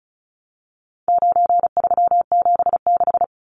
Morse code, heard as a keyed tone, about 34 wpm, 700 Hz. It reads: 9376
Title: Numbers - 4-Digit 35wpm